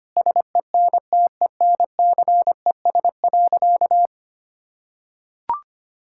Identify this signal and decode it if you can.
Morse code, heard as a keyed tone, about 25 wpm, 700 Hz. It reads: SENTENCES.
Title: Pangram Challenge v2 - 25wpm